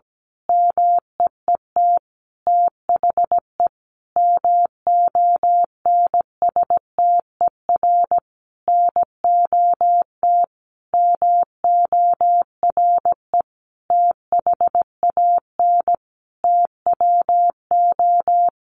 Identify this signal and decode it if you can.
Morse code, heard as a keyed tone, about 17 wpm, 700 Hz. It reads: MEET THE MONSTER NOT MORE THAN TWO
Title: A Sound Of Thunder 17wpm